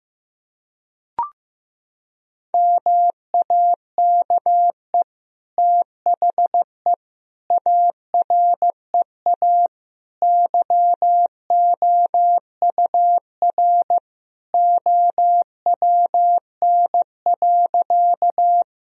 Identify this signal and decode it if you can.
Morse code, heard as a keyed tone, about 16 wpm, 700 Hz. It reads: MAKE THE AREA YOUR OWN.
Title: Sentences from Top 1000 Words - Review Part 2 - 15wpm